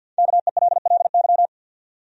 Morse code, heard as a keyed tone, about 51 wpm, 700 Hz. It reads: KELLY
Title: USA Common Names - Max 5 Letters - 50wpm